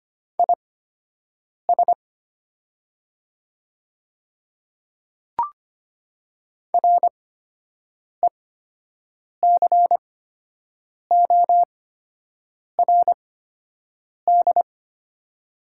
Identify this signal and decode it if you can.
Morse code, heard as a keyed tone, about 25 wpm, spaced as for 7 wpm, 700 Hz. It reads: IS RECORD
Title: 8x Character Spacing - Top 500 Words - 25wpm